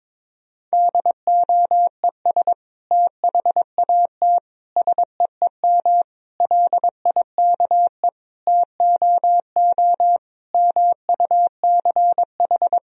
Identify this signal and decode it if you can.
Morse code, heard as a keyed tone, about 22 wpm, 700 Hz. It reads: DOES THAT SEEM LIKE TOO MUCH
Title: Sentences from Top 1000 Words - Review Part 1 - 22wpm